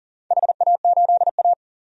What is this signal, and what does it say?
Morse code, heard as a keyed tone, about 40 wpm, 700 Hz. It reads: HA8A
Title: DXCC-100 Call Signs - 4 Characters - 40wpm